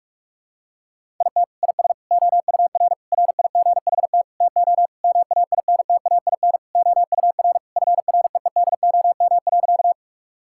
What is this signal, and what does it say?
Morse code, heard as a keyed tone, about 46 wpm, 700 Hz. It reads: IT IS OUR RIGHT TO MAINTAIN OUR FREEDOM.
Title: Sentences from Top 1000 Words - Review Part 5 - 45wpm